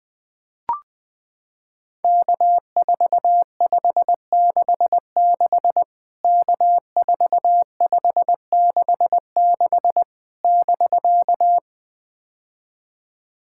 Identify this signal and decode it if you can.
Morse code, heard as a keyed tone, about 20 wpm, 700 Hz. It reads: K4566 K4566 <BK>
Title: POTA QSO Practice - 20wpm